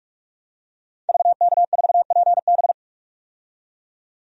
Morse code, heard as a keyed tone, about 43 wpm, 700 Hz. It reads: VK4PB
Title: DXCC-100 Call Signs - 5 Characters - 45wpm